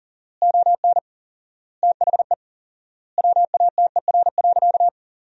Morse code, heard as a keyed tone, about 40 wpm, 700 Hz. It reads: ON THE WATER.
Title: 4x Word-Spacing - Sentences from Top 500 Words - 40wpm